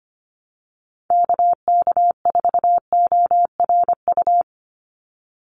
Morse code, heard as a keyed tone, about 25 wpm, 690 Hz. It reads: KK4ORU
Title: USA Call Signs 25wpm